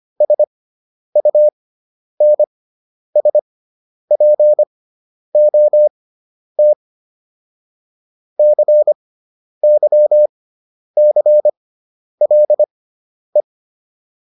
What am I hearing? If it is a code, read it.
Morse code, heard as a keyed tone, about 25 wpm, spaced as for 10 wpm, 600 Hz. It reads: SUNSPOT CYCLE